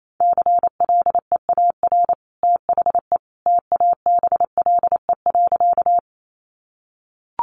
Morse code, heard as a keyed tone, about 28 wpm, 700 Hz. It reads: CLEAR THE TABLE.
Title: Sentences from Top 500 Words 28wpm